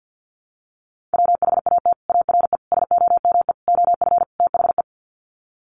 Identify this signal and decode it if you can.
Morse code, heard as a keyed tone, about 50 wpm, 700 Hz. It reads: WHAT ARE SOME OF THE
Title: Sets of 5 Words Encore 50wpm